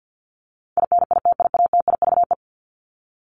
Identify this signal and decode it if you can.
Morse code, heard as a keyed tone, about 50 wpm, 700 Hz. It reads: INITIATIVE
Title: Top 1751-2000 Words 50wpm